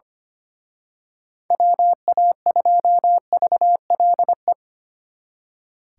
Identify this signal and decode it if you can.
Morse code, heard as a keyed tone, about 29 wpm, 700 Hz. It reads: WA2VLE
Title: Speed Racing - USA 2x3 Call Signs 25wpm